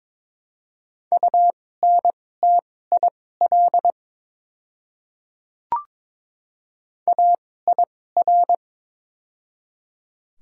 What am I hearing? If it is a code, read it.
Morse code, heard as a keyed tone, about 22 wpm, spaced as for 16 wpm, 700 Hz. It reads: UNTIL AIR
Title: 2x Character Spacing - Top 300 Words - 22wpm